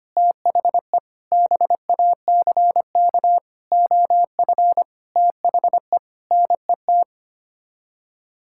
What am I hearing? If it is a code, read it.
Morse code, heard as a keyed tone, about 25 wpm, 700 Hz. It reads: THE BACK OF THE NET
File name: Sets of 9 Words - 25wpm